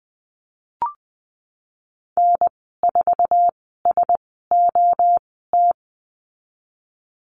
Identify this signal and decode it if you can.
Morse code, heard as a keyed tone, about 20 wpm, spaced as for 15 wpm, 700 Hz. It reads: N4SOT
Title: 082 Lesson 14g - Number 2 - Call Signs 20wpm